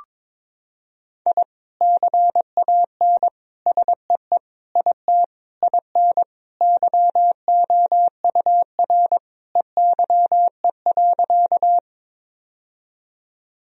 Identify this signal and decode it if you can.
Morse code, heard as a keyed tone, about 22 wpm, 700 Hz. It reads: I CAN SEE IT IN YOUR EYE.
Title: Sentences from Top 1000 Words - Review Part 1 - 22wpm